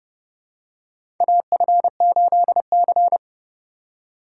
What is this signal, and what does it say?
Morse code, heard as a keyed tone, about 30 wpm, 700 Hz. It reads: AF8C
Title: USA Call Signs - 4 Characters - 4x Character Spacing - 30wpm